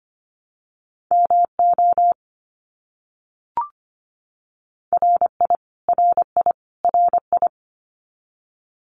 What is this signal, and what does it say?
Morse code, heard as a keyed tone, about 25 wpm, 700 Hz. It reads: MO RI RI RI
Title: US States Repeated 3x 25wpm